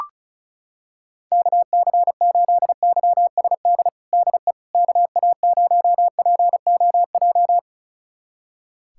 Morse code, heard as a keyed tone, about 36 wpm, 700 Hz. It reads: KC8YSD DE KA0POJ
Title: USA Callsign DE Callsign - 6 Characters - 35wpm